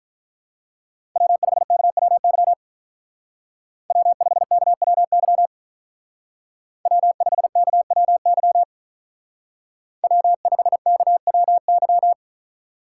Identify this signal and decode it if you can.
Morse code, heard as a keyed tone, about 43 wpm, 700 Hz. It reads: W5KWY W5KWY W5KWY W5KWY
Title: Speed Racing - USA Call Signs 35wpm